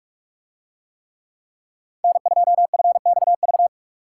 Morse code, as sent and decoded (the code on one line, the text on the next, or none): -. ..--- ..-. -..- ...-
N2FXV